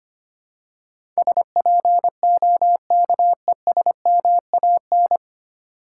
Morse code, heard as a keyed tone, about 25 wpm, 700 Hz. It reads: SPOKESMAN
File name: Nine Letter Words - 25wpm